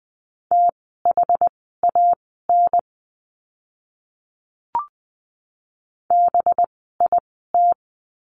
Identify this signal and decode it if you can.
Morse code, heard as a keyed tone, about 20 wpm, spaced as for 14 wpm, 700 Hz. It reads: THAN BIT
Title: 185 Lesson 30f - Letter b - Words 20wpm